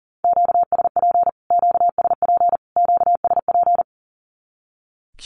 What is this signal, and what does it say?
Morse code, heard as a keyed tone, about 40 wpm, 700 Hz. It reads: QSP QSP QSP
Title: Q-Codes Repeated 3x 40wpm